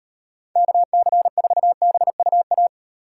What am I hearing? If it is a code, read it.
Morse code, heard as a keyed tone, about 38 wpm, 700 Hz. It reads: KC4BUA